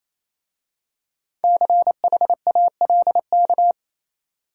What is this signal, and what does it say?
Morse code, heard as a keyed tone, about 27 wpm, 700 Hz. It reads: CHALK
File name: Easy Words - Speed-Racing with Hints - 22wpm